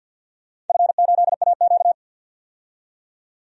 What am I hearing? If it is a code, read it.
Morse code, heard as a keyed tone, about 53 wpm, 700 Hz. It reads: F8AQ
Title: DXCC-100 Call Signs - 4 Characters - 50wpm